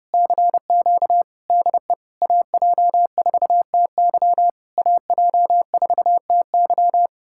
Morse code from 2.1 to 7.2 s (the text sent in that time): AJ4TY AJ4TY